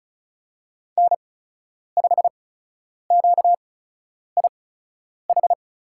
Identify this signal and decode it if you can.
Morse code, heard as a keyed tone, about 34 wpm, spaced as for 10 wpm, 700 Hz. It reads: N5QIH
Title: USA Call Signs - 5 Characters - 8x Character Spacing - 35wpm